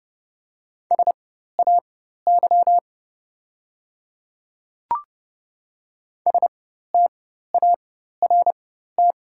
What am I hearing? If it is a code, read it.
Morse code, heard as a keyed tone, about 30 wpm, spaced as for 14 wpm, 700 Hz. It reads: SAY START
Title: 4x Character Spacing - Top 300 Words - 30wpm